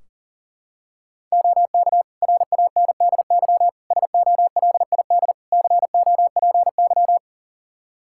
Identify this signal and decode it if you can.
Morse code, heard as a keyed tone, about 40 wpm, 700 Hz. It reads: OK RANDY SOLID COPY
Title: Practice QSO - Exchange 3 - More Info - 40wpm